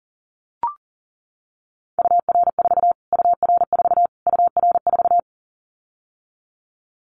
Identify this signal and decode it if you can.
Morse code, heard as a keyed tone, about 40 wpm, 700 Hz. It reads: UR4 UR4 UR4
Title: Callsign Prefixes for Top 25 DXCC - Repeated 3x - 40wpm